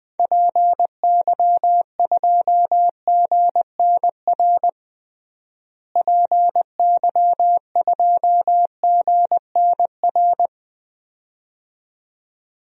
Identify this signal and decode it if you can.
Morse code, heard as a keyed tone, about 20 wpm, 700 Hz. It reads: PY2GNR PY2GNR
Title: DXCC-100 Call Signs - 6 Characters - Repeated 2x - 20wpm